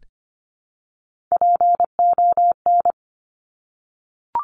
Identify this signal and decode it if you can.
Morse code, heard as a keyed tone, about 25 wpm, 700 Hz. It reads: PON
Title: Top Three Letter Combinations in Words 25wpm